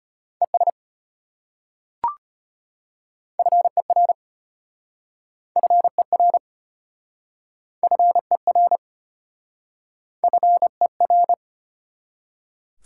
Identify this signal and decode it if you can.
Morse code, heard as a keyed tone, about 33 wpm, 690 Hz. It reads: ES FER FER FER FER